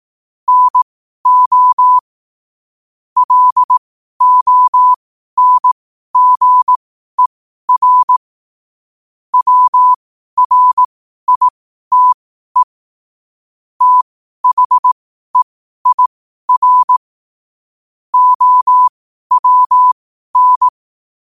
Morse code, heard as a keyed tone, about 18 wpm, spaced as for 12 wpm, 1000 Hz. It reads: NO LONGER WRITE THEIR OWN
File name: qotdfiles / 12wpm